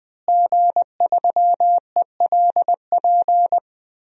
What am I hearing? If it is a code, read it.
Morse code, heard as a keyed tone, about 20 wpm, 700 Hz. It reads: G3ELP
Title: DXCC-100 Call Signs - 5 Characters - Repeated 3x - 20wpm